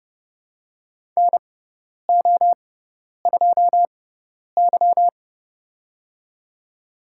Morse code, heard as a keyed tone, about 31 wpm, spaced as for 10 wpm, 700 Hz. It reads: NO2Y
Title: USA Call Signs - 6x Character Spacing - 30wpm